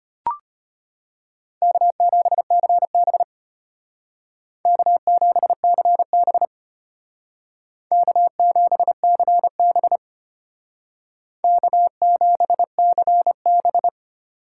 Morse code, heard as a keyed tone, about 33 wpm, 700 Hz. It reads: K7CB K7CB K7CB K7CB